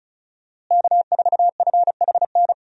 Morse code, heard as a keyed tone, about 36 wpm, 700 Hz. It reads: K4FHN
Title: USA Call Signs - 5 Characters - Repeated 2x - 35wpm